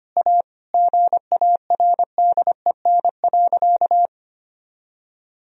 Morse code, heard as a keyed tone, about 25 wpm, 700 Hz. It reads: A GARDEN.
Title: Sentences from Top 1000 Words - Review Part 4 - 25wpm